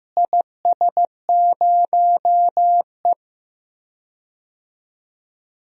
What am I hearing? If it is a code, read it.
Morse code, heard as a keyed tone, about 15 wpm, 700 Hz. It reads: IS0E
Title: DXCC-100 Call Signs - 4 Characters - Repeated 2x - 15wpm